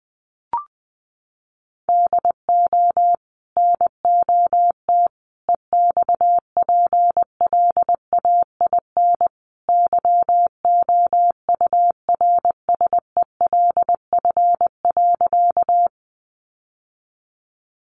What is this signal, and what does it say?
Morse code, heard as a keyed tone, about 20 wpm, 700 Hz. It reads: DO NOT EXPLAIN YOURSELF.
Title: Sentences from Top 1100 Words 20wpm